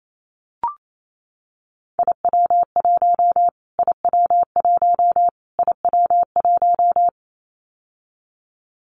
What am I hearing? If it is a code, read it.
Morse code, heard as a keyed tone, about 28 wpm, 700 Hz. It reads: IW1 IW1 IW1
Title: Callsign Prefixes for Top 25 DXCC - Repeated 3x - 28wpm